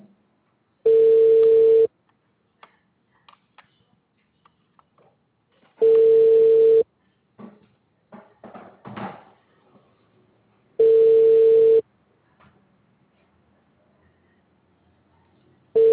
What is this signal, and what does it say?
Ringback tone (China, 450 Hz).